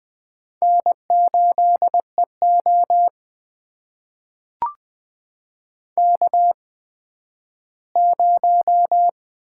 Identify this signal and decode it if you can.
Morse code, heard as a keyed tone, about 20 wpm, spaced as for 12 wpm, 700 Hz. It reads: N8EO K0
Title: USA Call Signs - 4 Characters - 8x Character Spacing - 20wpm